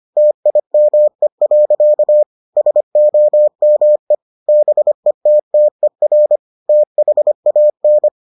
Morse code, heard as a keyed tone, about 25 wpm, 600 Hz. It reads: TIME. SOME BETTER THAN